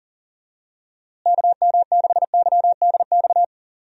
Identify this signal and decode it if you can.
Morse code, heard as a keyed tone, about 37 wpm, 700 Hz. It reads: KM6YDX
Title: USA Call Signs - 6 Characters - Repeated 3x - 40wpm